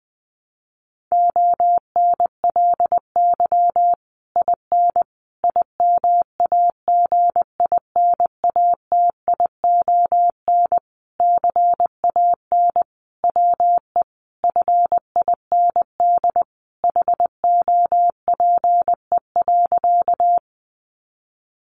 Morse code, heard as a keyed tone, about 20 wpm, 700 Hz. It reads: ONLY IN IMAGINATION CAN WE FIND HOPE.
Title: Sentences from Top 1300 Words 20wpm